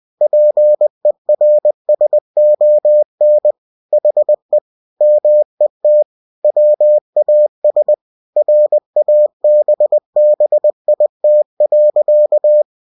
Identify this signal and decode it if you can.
Morse code, heard as a keyed tone, about 20 wpm, 600 Hz. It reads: PERSON HE MET WAS RABBIT.
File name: cw-books / winnie-the-pooh / winnie-20wpm-page-0077